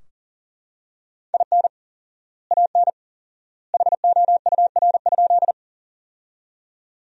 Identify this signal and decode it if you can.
Morse code, heard as a keyed tone, about 40 wpm, 700 Hz. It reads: IN AN HOUR?
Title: 4x Word-Spacing - Sentences from Top 400 Words - 40wpm